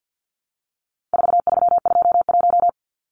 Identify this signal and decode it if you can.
Morse code, heard as a keyed tone, about 50 wpm, 700 Hz. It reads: 4321
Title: Numbers - 4-Digit 50wpm